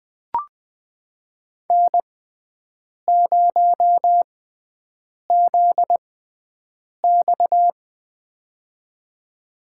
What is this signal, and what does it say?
Morse code, heard as a keyed tone, about 21 wpm, spaced as for 7 wpm, 700 Hz. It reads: N0ZX
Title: USA Call Signs - 4 Characters - 6x Character Spacing - 20wpm